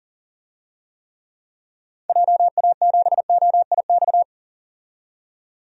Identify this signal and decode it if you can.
Morse code, heard as a keyed tone, about 40 wpm, 700 Hz. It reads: JA7OIX
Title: DXCC-100 Call Signs - 6 Characters - 40wpm